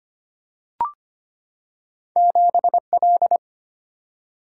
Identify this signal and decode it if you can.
Morse code, heard as a keyed tone, about 25 wpm, 700 Hz. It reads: 7L
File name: Two Letter-Number Combinations - 25wpm